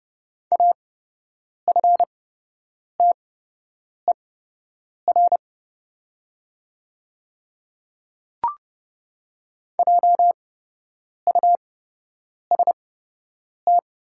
Morse code, heard as a keyed tone, about 28 wpm, spaced as for 8 wpm, 690 Hz. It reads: AFTER JUST
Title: 8x Character Spacing - Top 200 Words - 30wpm